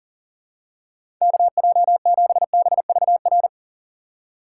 Morse code, heard as a keyed tone, about 40 wpm, 700 Hz. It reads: KJ7BVR
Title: USA Call Signs - 6 Characters - 4x Character Spacing - 40wpm